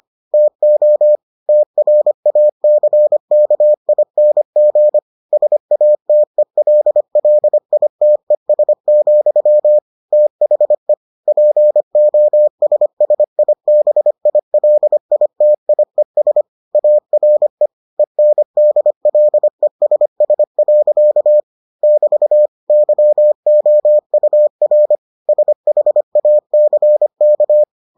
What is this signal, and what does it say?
Morse code, heard as a keyed tone, about 25 wpm, 600 Hz. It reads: TO TRACKING SATELLITES, THE POSSIBILITIES ARE ENDLESS. <BT> YOUR SHACK